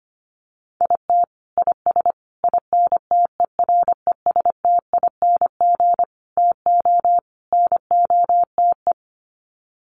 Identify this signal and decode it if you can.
Morse code, heard as a keyed tone, about 25 wpm, 700 Hz. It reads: IT IS INTERESTING TO NOTE
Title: Sets of 5 Words Encore2 25wpm